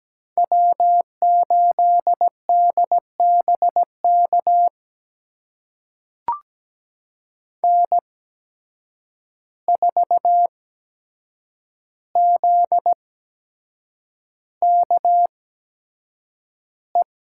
Morse code, heard as a keyed tone, about 17 wpm, spaced as for 7 wpm, 700 Hz. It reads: W8DBK N4ZKE
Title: USA Call Signs - 5 Characters - 8x Character Spacing - 17wpm